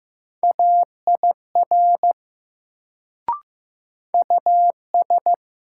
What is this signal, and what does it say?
Morse code, heard as a keyed tone, about 15 wpm, 700 Hz. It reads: AIR US
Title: Sentences from Top 200 Words - New Word Drill - 15wpm